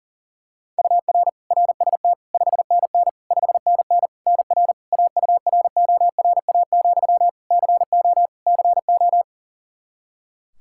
Morse code, heard as a keyed tone, about 40 wpm, 700 Hz. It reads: UR RST 5NN 5NN NR AURORA, CO CO
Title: Practice QSO - Exchange 1 - Beginning - 40wpm